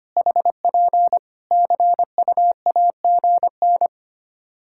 Morse code, heard as a keyed tone, about 25 wpm, 700 Hz. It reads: HP CUAGN
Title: Practice QSO - Exchange 4 - Ending - 25wpm